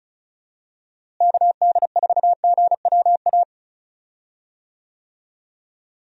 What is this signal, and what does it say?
Morse code, heard as a keyed tone, about 36 wpm, 700 Hz. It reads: KD4GWA
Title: USA Call Signs - 6 Characters - Repeated 2x - 35wpm